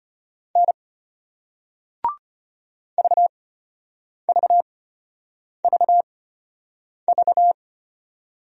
Morse code, heard as a keyed tone, about 33 wpm, spaced as for 8 wpm, 700 Hz. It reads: N VVVV